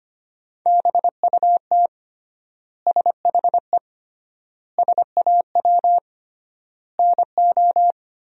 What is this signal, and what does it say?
Morse code, heard as a keyed tone, about 25 wpm, 700 Hz. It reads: BUT SHE SAW NO
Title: 3x Word-Spacing - Sentences from Top 300 Words - 25wpm